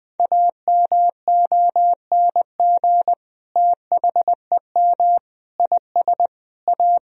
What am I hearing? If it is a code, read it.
Morse code, heard as a keyed tone, about 20 wpm, 700 Hz. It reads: AMONG THEM IS A
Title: Sentences from Top 1000 Words - Review Part 3 - 20wpm